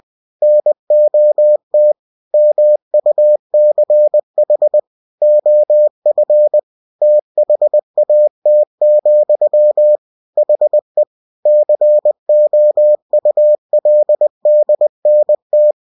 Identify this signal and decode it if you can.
Morse code, heard as a keyed tone, about 20 wpm, 600 Hz. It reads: NOT MUCH OF THAT, HE COULDNT